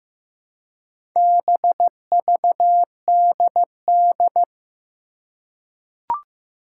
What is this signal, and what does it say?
Morse code, heard as a keyed tone, about 15 wpm, 700 Hz. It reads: BVDD
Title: Four Challenging Letter-Number Combinations 15wpm